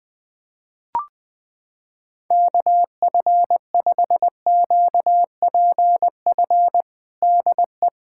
Morse code, heard as a keyed tone, about 20 wpm, 700 Hz. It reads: KF5QPF DE
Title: USA Callsign DE Callsign 20wpm